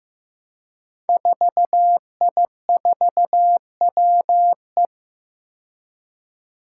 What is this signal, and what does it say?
Morse code, heard as a keyed tone, about 15 wpm, 700 Hz. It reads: 4I4WE
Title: Five Challenging Letter-Number Combinations 15wpm